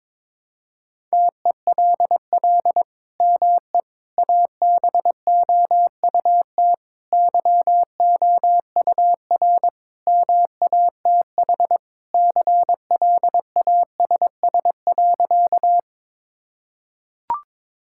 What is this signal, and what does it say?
Morse code, heard as a keyed tone, about 21 wpm, 700 Hz. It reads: TELL ME ABOUT YOUR MATH CLASS.
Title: Sentences from Top 1000 Words - Review Part 6 - 22wpm